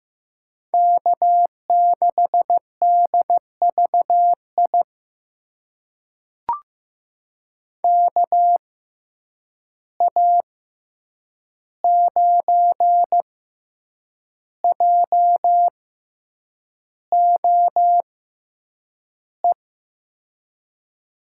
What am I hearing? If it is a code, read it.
Morse code, heard as a keyed tone, about 15 wpm, spaced as for 7 wpm, 700 Hz. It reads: K6DVI KA9JOE
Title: USA Call Signs - 6x Character Spacing - 15wpm